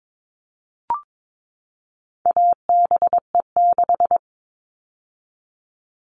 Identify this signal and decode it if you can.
Morse code, heard as a keyed tone, about 22 wpm, 700 Hz. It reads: ABE6
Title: All Cut Numbers - 4-Digit 22wpm